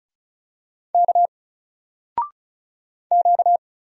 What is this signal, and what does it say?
Morse code, heard as a keyed tone, about 36 wpm, 700 Hz. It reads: K Q
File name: Single-Letter-Number-35wpm